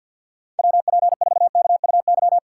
Morse code, heard as a keyed tone, about 52 wpm, 700 Hz. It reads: WP4KUY
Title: USA Call Signs - 6x Character Spacing - 50wpm